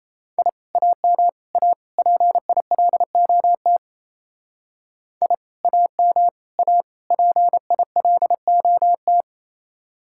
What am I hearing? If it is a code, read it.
Morse code, heard as a keyed tone, about 35 wpm, 700 Hz. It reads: I AM A PILOT I AM A PILOT
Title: Speed-Racing Common QSO Phrases - 28wpm